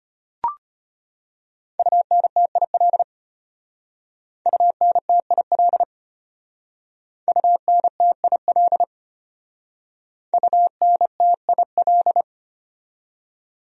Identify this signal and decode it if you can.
Morse code, heard as a keyed tone, about 31 wpm, 700 Hz. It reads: UNTIL UNTIL UNTIL UNTIL
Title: Speed-Racing Top 500 Words 25wpm